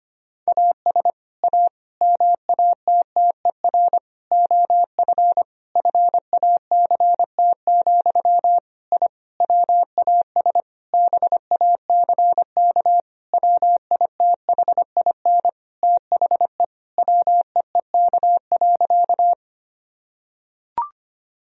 Morse code, heard as a keyed tone, about 25 wpm, 700 Hz. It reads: AS A MATTER OF FACT, I WAS BACK WITHIN THE WEEK.
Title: Sentences from Top 1000 Words - Review Part 4 - 25wpm